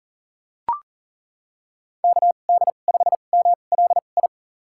Morse code, heard as a keyed tone, about 40 wpm, spaced as for 29 wpm, 700 Hz. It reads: KD5MLI
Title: USA Call Signs - 6 Characters - 2x Character Spacing - 40wpm